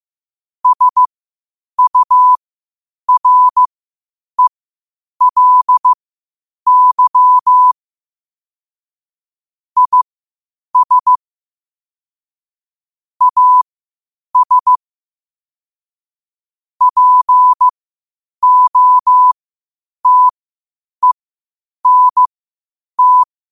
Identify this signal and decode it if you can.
Morse code, heard as a keyed tone, about 15 wpm, spaced as for 8 wpm, 1000 Hz. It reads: SURELY IS AS POTENT
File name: qotdfiles / 8wpm